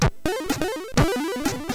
There is no code.